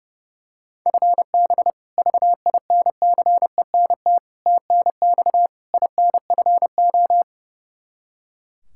Morse code, heard as a keyed tone, about 32 wpm, 700 Hz. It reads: FB VINCENT TNX INFO
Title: Practice QSO - Exchange 3 - More Info - 30wpm